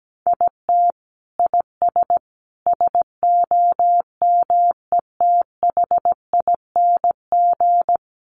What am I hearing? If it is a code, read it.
Morse code, heard as a keyed tone, about 17 wpm, 700 Hz. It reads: IT IS SOMETHING
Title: Sentences from Top 1100 Words 17wpm